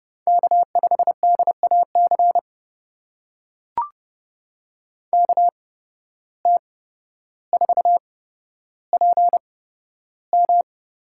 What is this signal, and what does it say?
Morse code, heard as a keyed tone, about 29 wpm, spaced as for 13 wpm, 700 Hz. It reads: K5DAC KT4PM
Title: USA Call Signs - 5 Characters - 8x Character Spacing - 30wpm